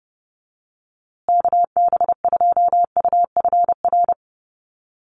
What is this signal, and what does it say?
Morse code, heard as a keyed tone, about 30 wpm, 700 Hz. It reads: KB2UFR
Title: USA Call Signs Repeated 3x 30wpm